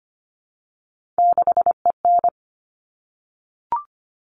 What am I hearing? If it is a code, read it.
Morse code, heard as a keyed tone, about 25 wpm, 700 Hz. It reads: 6EN
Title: All Cut Numbers - 3-Digit 25wpm